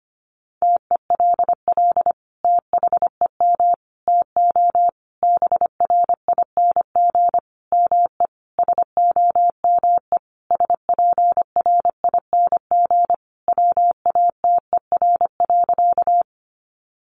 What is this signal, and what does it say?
Morse code, heard as a keyed tone, about 25 wpm, 700 Hz. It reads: TELL THEM TO BRING ME SOME SPRING WATER.
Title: Sentences from Top 1200 Words 25wpm